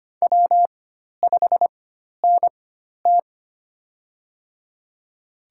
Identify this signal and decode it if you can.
Morse code, heard as a keyed tone, about 25 wpm, spaced as for 12 wpm, 700 Hz. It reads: W5NT
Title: USA Call Signs - 4 Characters - 4x Character Spacing - 25wpm